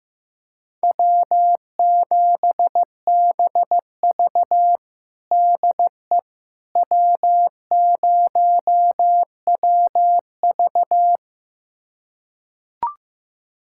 Morse code, heard as a keyed tone, about 15 wpm, 700 Hz. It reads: W7BV DE W0WV
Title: USA Callsign DE Callsign - 4 Characters - 15wpm